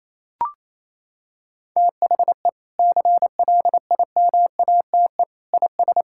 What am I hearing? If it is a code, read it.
Morse code, heard as a keyed tone, about 28 wpm, 700 Hz. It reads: THE CLIMATE IS
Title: Sentences from Top 1600 Words 28wpm